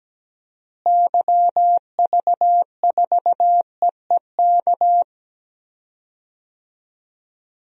Morse code, heard as a keyed tone, about 17 wpm, 700 Hz. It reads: YV4EEK
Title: DXCC-100 Call Signs - 6 Characters - Repeated 2x - 17wpm